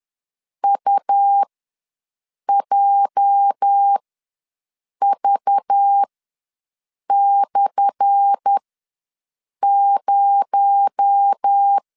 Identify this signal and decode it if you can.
Morse code, heard as a keyed tone, about 10 wpm, spaced as for 6 wpm, 800 Hz. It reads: UJV/0
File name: LEZIONI / 10 SIGNS / 10C DOT ws 10wpm 4213